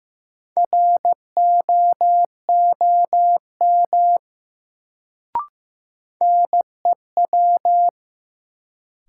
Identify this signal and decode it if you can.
Morse code, heard as a keyed tone, about 15 wpm, 700 Hz. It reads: ROOM NEW
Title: Sentences from Top 300 Words - New Word Drill - 15wpm